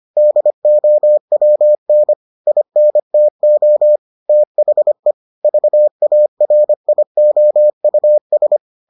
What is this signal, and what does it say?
Morse code, heard as a keyed tone, about 25 wpm, 600 Hz. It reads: DOWN INTO THE VARIOUS